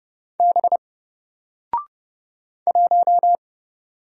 Morse code, heard as a keyed tone, about 30 wpm, 700 Hz. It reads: B 1
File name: Single-Letter-Number-30wpm